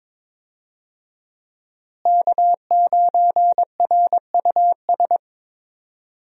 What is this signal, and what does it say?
Morse code, heard as a keyed tone, about 22 wpm, 700 Hz. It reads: K9RUS
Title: USA Call Signs - 5 Characters - Repeated 3x - 22wpm